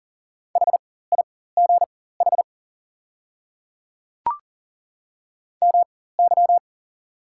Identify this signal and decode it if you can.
Morse code, heard as a keyed tone, about 40 wpm, spaced as for 19 wpm, 690 Hz. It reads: HIGH MY
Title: 4x Character Spacing - Top 200 Words - 40wpm